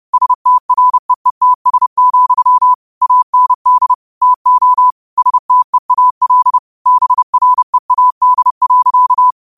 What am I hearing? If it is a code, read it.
Morse code, heard as a keyed tone, about 30 wpm, 1000 Hz. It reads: STREETS, AND TO STEAL BREAD.
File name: qotdfiles / 30wpm